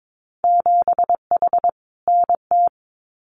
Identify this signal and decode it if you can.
Morse code, heard as a keyed tone, about 22 wpm, 700 Hz. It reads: 7H NT
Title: Winter Field Day Exchanges Encore 22wpm